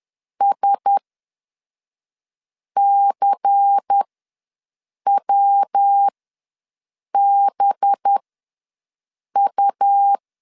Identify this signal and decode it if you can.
Morse code, heard as a keyed tone, about 10 wpm, spaced as for 5 wpm, 800 Hz. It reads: SCWBU
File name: LEZIONI / 5 LBVC / 5 LBVC ws 10wpm 3733